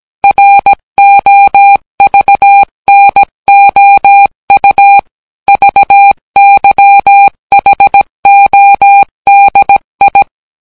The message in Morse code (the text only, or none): ROVNOU VYHODI